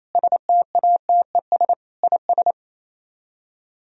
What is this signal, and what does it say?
Morse code, heard as a keyed tone, about 36 wpm, 700 Hz. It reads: STATES IS